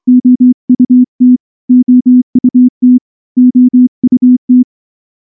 Morse code, heard as a keyed tone, about 24 wpm, 250 Hz. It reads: OUT OUT OUT
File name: morse / morse-training-25WPM